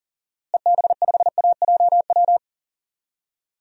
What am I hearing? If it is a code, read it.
Morse code, heard as a keyed tone, about 40 wpm, 700 Hz. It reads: EB5AJW
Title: DXCC-100 Call Signs - 6 Characters - Repeated 3x - 40wpm